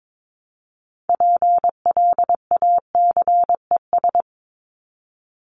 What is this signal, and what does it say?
Morse code, heard as a keyed tone, about 22 wpm, 700 Hz. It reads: PLACES
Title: ES:S Suffix - Top 100 Words with Base Words 22wpm